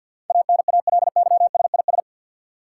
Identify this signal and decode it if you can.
Morse code, heard as a keyed tone, about 50 wpm, 700 Hz. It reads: ANALYSIS
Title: Sentences from Top 1400 Words - New Word Drill Alt - 50wpm